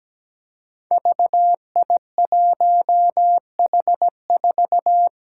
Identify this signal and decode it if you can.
Morse code, heard as a keyed tone, about 17 wpm, 700 Hz. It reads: VI1H4
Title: Five Challenging Letter-Number Combinations 17wpm